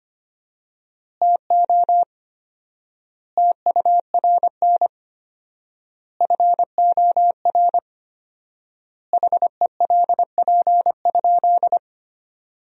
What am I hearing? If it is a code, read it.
Morse code, heard as a keyed tone, about 25 wpm, 700 Hz. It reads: TO TURN FOR HELP?
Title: 4x Word-Spacing - Sentences from Top 200 Words - 25wpm